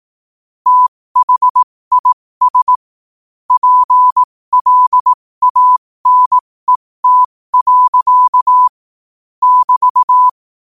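Morse code, heard as a keyed tone, about 18 wpm, spaced as for 15 wpm, 1000 Hz. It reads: THIS PLANET. <BT>
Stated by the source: Morse practice sent at 15 wpm.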